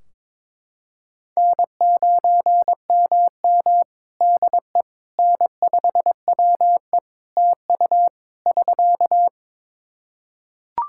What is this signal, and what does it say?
Morse code, heard as a keyed tone, about 21 wpm, 700 Hz. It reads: N9MM DE N5WE TU <SK>
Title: Practice QSO - Exchange 4 - Ending - 22wpm